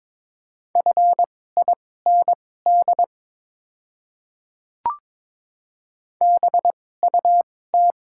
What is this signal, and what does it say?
Morse code, heard as a keyed tone, about 22 wpm, spaced as for 16 wpm, 700 Hz. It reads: FIND BUT
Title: 2x Character Spacing - Top 200 Words - 22wpm